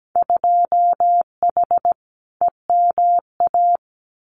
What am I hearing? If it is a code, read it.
Morse code, heard as a keyed tone, about 17 wpm, 700 Hz. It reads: 2H EMA
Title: Winter Field Day Exchanges 17wpm